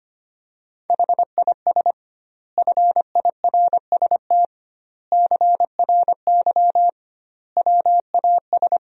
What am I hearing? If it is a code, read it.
Morse code, heard as a keyed tone, about 25 wpm, 700 Hz. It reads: HIS FIRST CRY WAS
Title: 2x Word-Spacing - Sentences from Top 500 Words - 25wpm